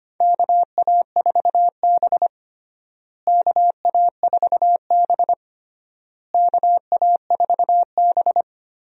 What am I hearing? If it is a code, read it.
Morse code, heard as a keyed tone, about 25 wpm, 700 Hz. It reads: KA4B KA4B KA4B
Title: USA Call Signs - 4 Characters - Repeated 3x - 25wpm